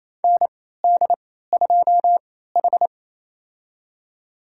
Morse code, heard as a keyed tone, about 28 wpm, spaced as for 16 wpm, 700 Hz. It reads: ND2H
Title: USA Call Signs - 4 Characters - 3x Character Spacing - 28wpm